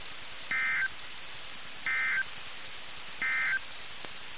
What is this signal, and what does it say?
Emergency Alert System (SAME) tones, heard as harsh buzzing bursts. It is heard from 0.4 to 3.7 s.
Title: NOAA Weather Radio - KIH61